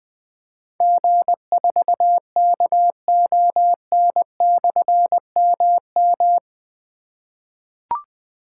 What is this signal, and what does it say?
Morse code, heard as a keyed tone, about 21 wpm, 700 Hz. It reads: G4KON/MM
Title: Difficult Call Signs 20wpm